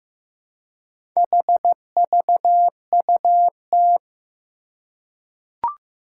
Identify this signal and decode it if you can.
Morse code, heard as a keyed tone, about 15 wpm, 700 Hz. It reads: HVUT
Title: Four Challenging Letter-Number Combinations 15wpm